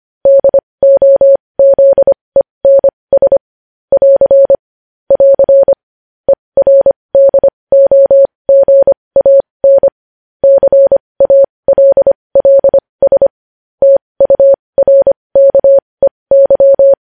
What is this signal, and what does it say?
Morse code, heard as a keyed tone, about 25 wpm, spaced as for 20 wpm, 550 Hz. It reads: DOZENS <AR> <AR> ERDOGAN CALLS TURKEY
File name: podcast / News headlines-1x-25x20